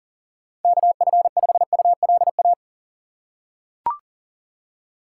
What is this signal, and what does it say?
Morse code, heard as a keyed tone, about 40 wpm, 700 Hz. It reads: KF5ULA
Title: USA Call Signs - 6 Characters - 2x Character Spacing - 40wpm